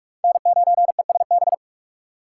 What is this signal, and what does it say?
Morse code, heard as a keyed tone, about 46 wpm, 700 Hz. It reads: N9ESB
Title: USA Call Signs - 2x Character Spacing - 45wpm